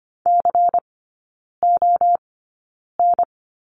Morse code, heard as a keyed tone, about 25 wpm, spaced as for 9 wpm, 700 Hz. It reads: CON
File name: Lesson 17e - Letter u - Three Characters 25wpm